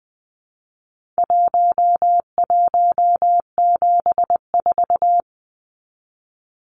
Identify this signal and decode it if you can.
Morse code, heard as a keyed tone, about 20 wpm, 700 Hz. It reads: 1174
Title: Numbers - Contest Style - 4-Digit 20wpm